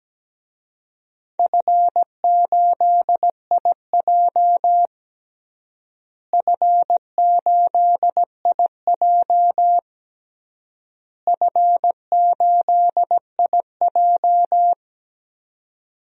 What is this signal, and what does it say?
Morse code, heard as a keyed tone, about 17 wpm, 700 Hz. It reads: F8IJ F8IJ F8IJ
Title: DXCC-100 Call Signs - 4 Characters - Repeated 3x - 17wpm